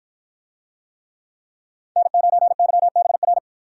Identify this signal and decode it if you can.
Morse code, heard as a keyed tone, about 53 wpm, 700 Hz. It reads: N9YBR